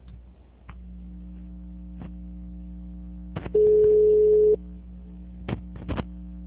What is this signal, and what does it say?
Ringback tone (Europe, 425 Hz).
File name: out- -2060- - - .27781